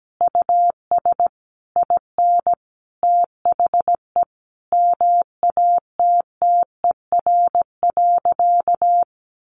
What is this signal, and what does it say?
Morse code, heard as a keyed tone, about 17 wpm, 700 Hz. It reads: US IN THE MATTER.
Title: Sentences from Top 1200 Words 17wpm